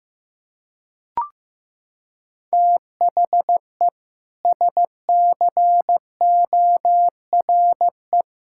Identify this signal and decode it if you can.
Morse code, heard as a keyed tone, about 15 wpm, 700 Hz. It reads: THE SCORE
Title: Sentences from Top 1600 Words 15wpm